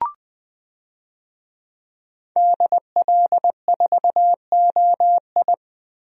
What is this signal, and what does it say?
Morse code, heard as a keyed tone, about 20 wpm, 700 Hz. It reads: DL4OI
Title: USA Call Signs - 5 Characters - Repeated 2x - 20wpm